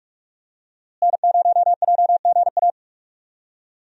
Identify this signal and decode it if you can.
Morse code, heard as a keyed tone, about 44 wpm, 700 Hz. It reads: N0JGA
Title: USA Call Signs - 5 Characters - Repeated 3x - 45wpm